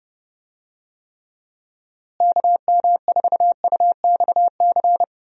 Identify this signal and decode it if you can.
Morse code, heard as a keyed tone, about 30 wpm, 700 Hz. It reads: KM4UXC
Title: USA Call Signs - 6 Characters - Repeated 2x - 30wpm